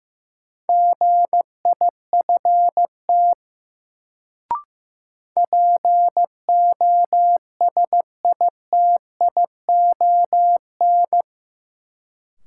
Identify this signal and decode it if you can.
Morse code, heard as a keyed tone, about 15 wpm, 700 Hz. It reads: GIFT POSITION
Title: Sentences from Top 900 Words - New Word Drill - 15wpm